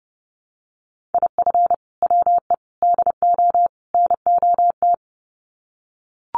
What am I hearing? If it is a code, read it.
Morse code, heard as a keyed tone, about 30 wpm, 700 Hz. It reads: IF WE DO NOT
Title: Sets of 4 Words 30wpm